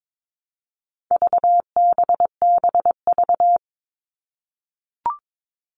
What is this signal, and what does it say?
Morse code, heard as a keyed tone, about 22 wpm, 700 Hz. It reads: VBBV
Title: All Cut Numbers - 4-Digit 22wpm